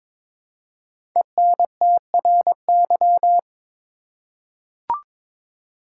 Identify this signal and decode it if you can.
Morse code, heard as a keyed tone, about 22 wpm, 700 Hz. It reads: ENTRY
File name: 5 Letter Words - First 3 Letters Given - 22wpm